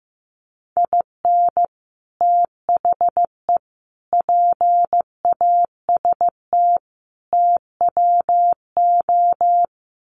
Morse code, heard as a keyed tone, about 15 wpm, 700 Hz. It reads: IN THE PAST TWO
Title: Sets of 5 Words 15wpm